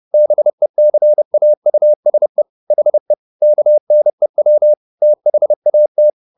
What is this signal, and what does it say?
Morse code, heard as a keyed tone, about 30 wpm, 600 Hz. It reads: BECAUSE HE KNEW THAT
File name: the old man and the sea / the old man and the sea 1 30wpm